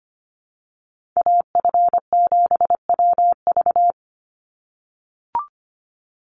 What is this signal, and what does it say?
Morse code, heard as a keyed tone, about 25 wpm, 700 Hz. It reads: AF7WV
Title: USA Call Signs 25wpm